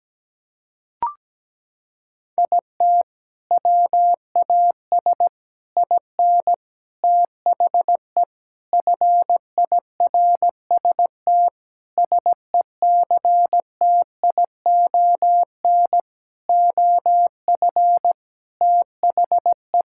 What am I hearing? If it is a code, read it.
Morse code, heard as a keyed tone, about 17 wpm, 700 Hz. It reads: IT WAS IN THE FIRST SECTION OF THE
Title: Sentences from Top 1000 Words - Review Part 5 - 17wpm